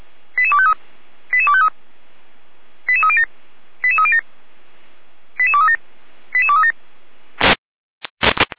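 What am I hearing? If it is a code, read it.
Five-tone selective call (ZVEI), heard as quick runs of five tones, tones of about 80 ms. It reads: 88253 88253 88307 88307 88237 88237
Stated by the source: ZVEI five-tone selective call 88209 received via SDR.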